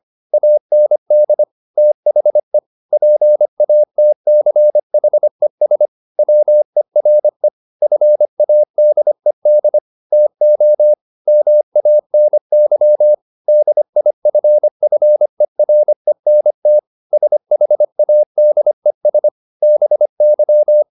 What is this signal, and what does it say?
Morse code, heard as a keyed tone, about 25 wpm, 600 Hz. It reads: AND THE PATCHES WERE FADED TO MANY DIFFERENT SHADES BY
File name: the old man and the sea part 1 25wpm